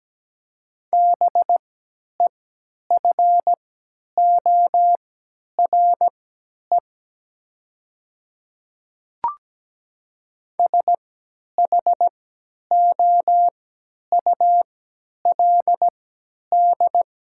Morse code, heard as a keyed tone, about 17 wpm, spaced as for 10 wpm, 700 Hz. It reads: BEFORE SHOULD
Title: 3x Character Spacing - Top 300 Words - 17wpm